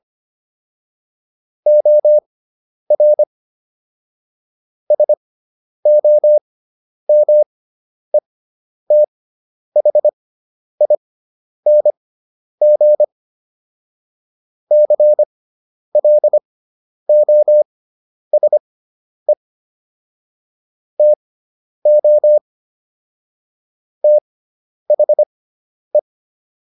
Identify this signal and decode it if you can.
Morse code, heard as a keyed tone, about 25 wpm, spaced as for 10 wpm, 600 Hz. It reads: OR SOMETHING CLOSE TO THE